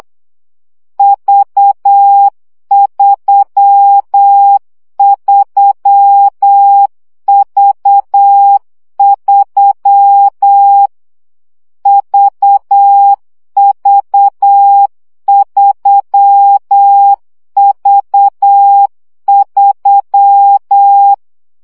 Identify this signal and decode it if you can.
Morse code, heard as a keyed tone, about 8 wpm, 800 Hz. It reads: V33V3 VV3V3
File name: CW gyakorlas v 3 7 baud